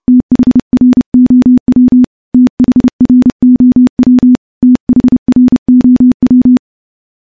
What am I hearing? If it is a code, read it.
Morse code, heard as a keyed tone, about 30 wpm, 250 Hz. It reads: THROW THROW THROW
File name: morse / morse-training-30WPM